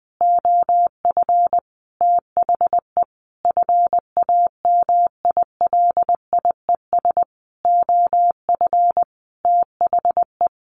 Morse code, heard as a keyed tone, about 20 wpm, 700 Hz. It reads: OF THE FAMILIES OF THE
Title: A Sound Of Thunder 20wpm